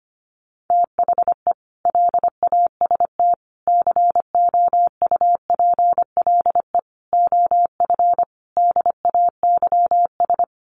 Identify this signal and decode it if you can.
Morse code, heard as a keyed tone, about 25 wpm, 700 Hz. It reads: THE LAST COUPLE OF DAYS
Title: Sets of 5 Words Encore 25wpm